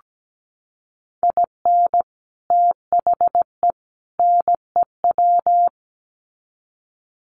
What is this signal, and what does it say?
Morse code, heard as a keyed tone, about 17 wpm, 700 Hz. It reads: IN THE NEW
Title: Sets of 3 Words 17wpm